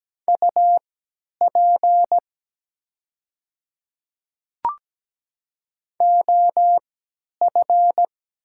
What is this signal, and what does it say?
Morse code, heard as a keyed tone, about 17 wpm, spaced as for 10 wpm, 700 Hz. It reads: UP OF
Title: 3x Character Spacing - Top 100 Words - 17wpm